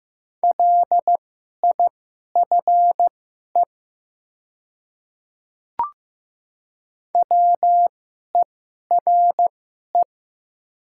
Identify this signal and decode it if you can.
Morse code, heard as a keyed tone, about 15 wpm, spaced as for 11 wpm, 700 Hz. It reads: LIFE WERE
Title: 2x Character Spacing - Top 500 Words - 15wpm